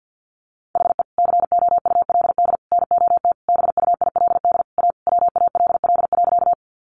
Morse code, heard as a keyed tone, about 50 wpm, 700 Hz. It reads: HE COULD NOT BUILD A WALL.
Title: Sentences from Top 1100 Words 50wpm